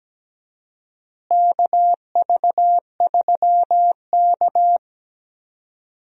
Morse code, heard as a keyed tone, about 17 wpm, 700 Hz. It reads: KV3K
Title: USA Call Signs - 4 Characters - Repeated 2x - 17wpm